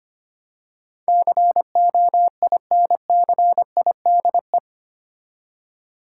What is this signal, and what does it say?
Morse code, heard as a keyed tone, about 25 wpm, 700 Hz. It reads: COINCIDE
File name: 8 Letter Words - First 3 Letters Given - 25wpm